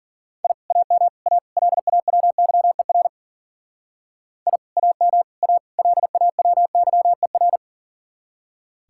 Morse code, heard as a keyed tone, about 52 wpm, 700 Hz. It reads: I AM A LAWYER I AM A LAWYER